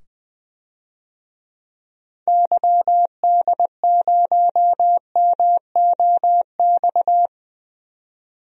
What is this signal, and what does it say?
Morse code, heard as a keyed tone, about 20 wpm, 700 Hz. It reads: YD0MOX
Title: DXCC-100 Call Signs - 6 Characters - 20wpm